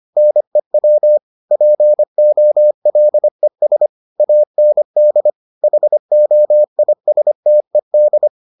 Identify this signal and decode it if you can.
Morse code, heard as a keyed tone, about 25 wpm, 600 Hz. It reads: NEW POLES AND HOISTED